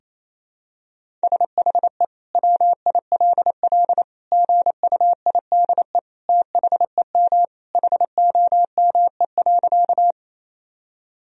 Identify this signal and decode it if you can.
Morse code, heard as a keyed tone, about 28 wpm, 700 Hz. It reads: SHE WILL GUIDE THEM HOME.
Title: Sentences from Top 1000 Words - Review Part 3 - 28wpm